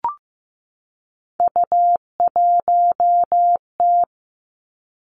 Morse code, heard as a keyed tone, about 15 wpm, 700 Hz. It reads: U1T
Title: Three Challenging Letter-Number Combinations 15wpm